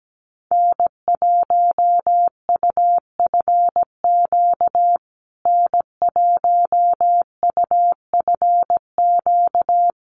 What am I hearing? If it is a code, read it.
Morse code, heard as a keyed tone, about 17 wpm, 700 Hz. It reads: N1UFQ N1UFQ
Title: USA Call Signs Repeated 3x 17wpm